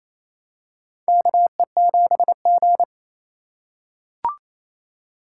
Morse code, heard as a keyed tone, about 28 wpm, 700 Hz. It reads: KE7G
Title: USA Call Signs - 6x Character Spacing - 28wpm